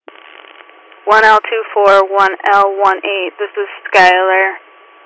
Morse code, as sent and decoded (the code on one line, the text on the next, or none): none